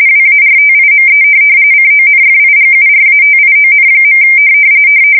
RTTY, about 45.45 baud. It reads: WARTIME CONFIGURATION.
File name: fsk170 / USS-Massachusetts